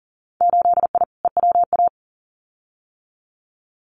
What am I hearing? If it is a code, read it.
Morse code, heard as a keyed tone, about 40 wpm, 700 Hz. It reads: 8I EWA
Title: Winter Field Day Exchanges Encore 40wpm